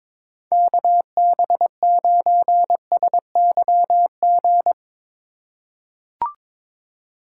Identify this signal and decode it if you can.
Morse code, heard as a keyed tone, about 22 wpm, 700 Hz. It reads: KB9SYG
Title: USA Call Signs - 6 Characters - 3x Character Spacing - 22wpm